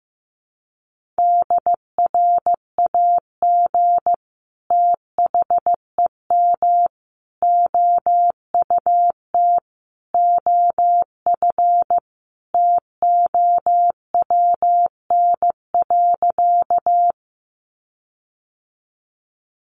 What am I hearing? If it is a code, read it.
Morse code, heard as a keyed tone, about 15 wpm, 700 Hz. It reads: DRAG THEM OUT OF TOWN.
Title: Sentences from Top 1500 Words 15wpm